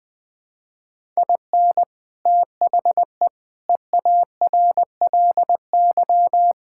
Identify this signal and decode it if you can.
Morse code, heard as a keyed tone, about 20 wpm, 700 Hz. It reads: IN THE EARLY
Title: Sentences with 7-Word Phrases - 20wpm